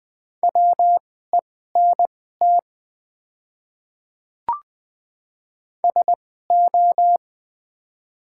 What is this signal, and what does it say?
Morse code, heard as a keyed tone, about 20 wpm, spaced as for 14 wpm, 700 Hz. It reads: WENT SO
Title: 2x Character Spacing - Top 300 Words - 20wpm